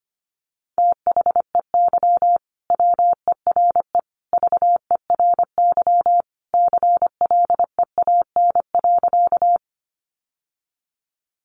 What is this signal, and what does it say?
Morse code, heard as a keyed tone, about 25 wpm, 700 Hz. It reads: THEY WERE VERY CLEAN.
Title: Sentences from Top 1100 Words 25wpm